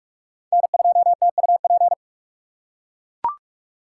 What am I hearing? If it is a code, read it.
Morse code, heard as a keyed tone, about 41 wpm, 700 Hz. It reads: N2TUP